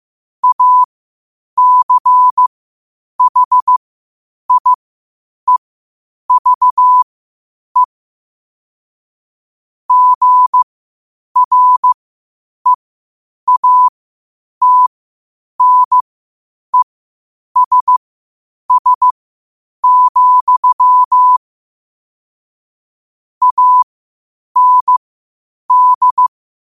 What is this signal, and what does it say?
Morse code, heard as a keyed tone, about 15 wpm, spaced as for 8 wpm, 1000 Hz. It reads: ACHIEVE GREATNESS, AND